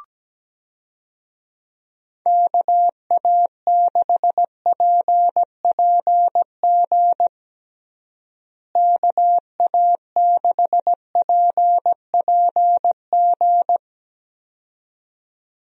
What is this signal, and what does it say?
Morse code, heard as a keyed tone, about 17 wpm, 700 Hz. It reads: KA6PPG KA6PPG
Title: USA Call Signs - 6 Characters - Repeated 2x - 17wpm